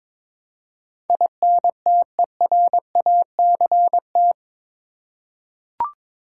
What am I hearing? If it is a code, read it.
Morse code, heard as a keyed tone, about 22 wpm, 700 Hz. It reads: INTERACT
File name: practice / 22wpm / Eight Letter Words - 22wpm